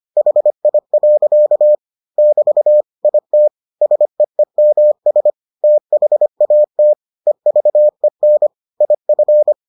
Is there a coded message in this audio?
Yes: Morse code from 0.2 to 9.5 s, heard as a keyed tone, about 25 wpm, 600 Hz. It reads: HI. <BT> IT SEEMS THAT EVEN IF